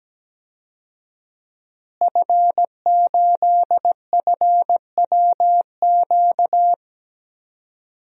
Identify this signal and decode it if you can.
Morse code, heard as a keyed tone, about 17 wpm, 700 Hz. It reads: F8FWQ
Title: DXCC-100 Call Signs - 5 Characters - Repeated 3x - 17wpm